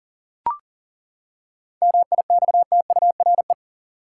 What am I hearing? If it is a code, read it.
Morse code, heard as a keyed tone, about 40 wpm, 700 Hz. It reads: MIXTURE
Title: Seven Letter Words - 40wpm